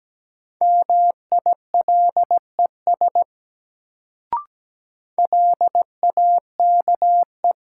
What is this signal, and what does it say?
Morse code, heard as a keyed tone, about 17 wpm, 700 Hz. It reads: MILES LAKE
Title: Sentences from Top 1000 Words - Review Part 3 - New Word Drill - 17wpm